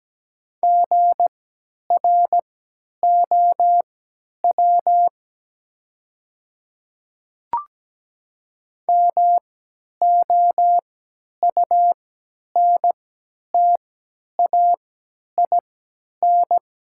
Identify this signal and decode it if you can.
Morse code, heard as a keyed tone, about 17 wpm, spaced as for 10 wpm, 700 Hz. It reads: GROW MOUNTAIN
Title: 3x Character Spacing - Top 300 Words - 17wpm